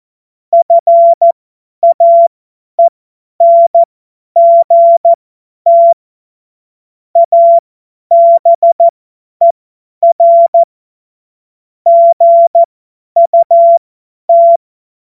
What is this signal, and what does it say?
Morse code, heard as a keyed tone, about 14 wpm, spaced as for 10 wpm, 680 Hz. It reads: FAENGT ABER GUT
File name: Morsekurs / Witze / Witz 20. Juli 2021, 50 F70